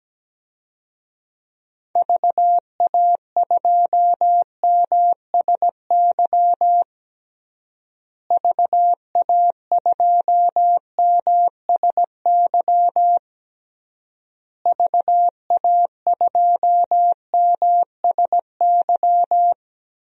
Morse code, heard as a keyed tone, about 17 wpm, 700 Hz. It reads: VA2MSY VA2MSY VA2MSY
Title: DXCC-100 Call Signs - 6 Characters - Repeated 3x - 17wpm